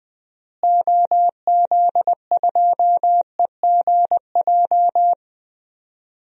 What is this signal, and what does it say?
Morse code, heard as a keyed tone, about 21 wpm, 700 Hz. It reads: OZ2EGJ
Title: DXCC-100 Call Signs - 6 Characters - Repeated 2x - 20wpm